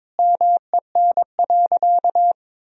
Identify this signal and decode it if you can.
Morse code, heard as a keyed tone, about 22 wpm, 700 Hz. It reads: MEN.
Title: Sentences from Top 1000 Words - Review Part 3 - 22wpm